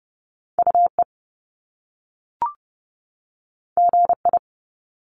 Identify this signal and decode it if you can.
Morse code, heard as a keyed tone, about 30 wpm, 700 Hz. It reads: UE GI
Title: Top Two Letter Combinations in Words 30wpm